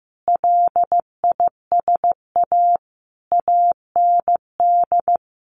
Morse code, heard as a keyed tone, about 15 wpm, 700 Hz. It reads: LISA AND
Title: Additional Easy Phrases 15wpm